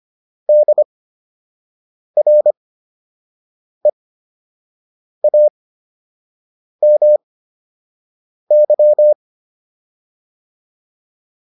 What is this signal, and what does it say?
Morse code, heard as a keyed tone, about 25 wpm, spaced as for 6 wpm, 600 Hz. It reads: DREAMY